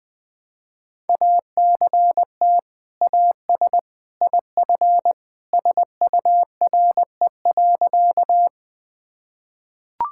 Morse code, heard as a keyed tone, about 20 wpm, 700 Hz. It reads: ACT AS IF SURE.
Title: Sentences with 2-Word Phrases - Spoken Beginning - 20wpm